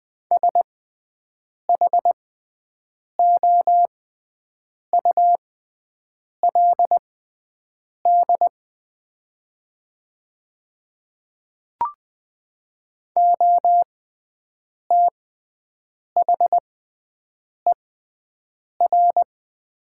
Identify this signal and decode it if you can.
Morse code, heard as a keyed tone, about 20 wpm, spaced as for 7 wpm, 700 Hz. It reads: SHOULD OTHER
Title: 6x Character Spacing - Top 300 Words - 20wpm